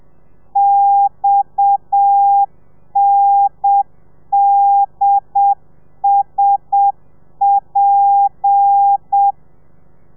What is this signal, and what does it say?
Morse code, heard as a keyed tone, about 7 wpm, 800 Hz. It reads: XNDSP
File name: telegrafia / i2tfj / Lez7 30